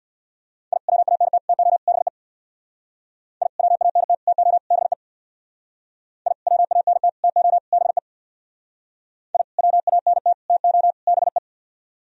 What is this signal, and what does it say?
Morse code, heard as a keyed tone, about 60 wpm, 690 Hz. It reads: I WANT TO BE I WANT TO BE I WANT TO BE I WANT TO BE
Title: Speed-Racing - Sets of 4 Words 50wpm